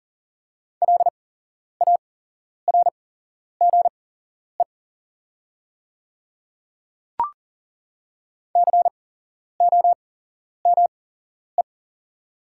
Morse code, heard as a keyed tone, about 40 wpm, spaced as for 11 wpm, 700 Hz. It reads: LARGE COME
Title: 8x Character Spacing - Top 200 Words - 40wpm